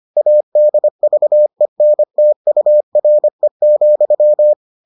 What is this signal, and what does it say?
Morse code, heard as a keyed tone, about 25 wpm, 600 Hz. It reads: ADVENTURE,